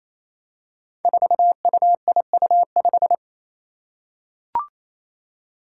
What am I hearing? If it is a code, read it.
Morse code, heard as a keyed tone, about 28 wpm, 700 Hz. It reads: 4UIU5
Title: Five Challenging Letter-Number Combinations 28wpm